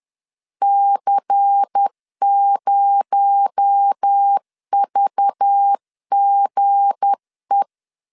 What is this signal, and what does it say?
Morse code, heard as a keyed tone, about 10 wpm, 800 Hz. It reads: C0VGE
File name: LEZIONI / 7 GF9Z / 7 G 10wpm 2501